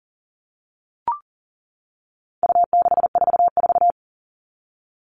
Morse code, heard as a keyed tone, about 40 wpm, 700 Hz. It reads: U644
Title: All Cut Numbers - 4-Digit 40wpm